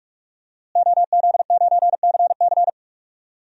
Morse code, heard as a keyed tone, about 45 wpm, 700 Hz. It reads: OZ9CC
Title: DXCC-100 Call Signs - 5 Characters - Repeated 2x - 45wpm